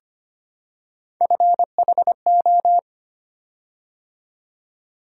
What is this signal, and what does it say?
Morse code, heard as a keyed tone, about 25 wpm, 700 Hz. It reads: FHO
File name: Three Letter-Number Combinations - 25wpm